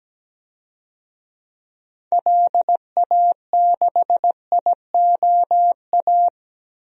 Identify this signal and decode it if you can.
Morse code, heard as a keyed tone, about 17 wpm, 700 Hz. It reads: LA6IOA
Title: DXCC-100 Call Signs - 6 Characters - 17wpm